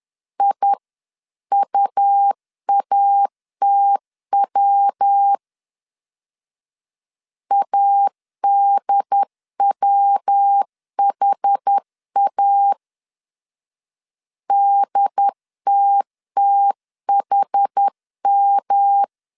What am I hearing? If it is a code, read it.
Morse code, heard as a keyed tone, about 10 wpm, 800 Hz. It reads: IUATW ADWHA DTTHM
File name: 10wpm